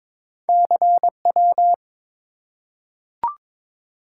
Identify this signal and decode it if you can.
Morse code, heard as a keyed tone, about 22 wpm, 700 Hz. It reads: CW
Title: Top 180 QSO Elements 22wpm